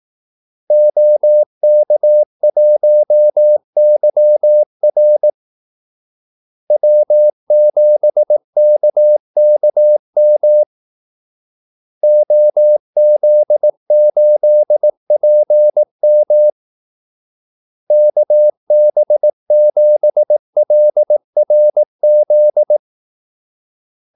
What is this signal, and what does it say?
Morse code, heard as a keyed tone, about 18 wpm, 600 Hz. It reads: OK1YR W7KKM OZ8PM KB7LRZ
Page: Kaldesignaler 18 wpm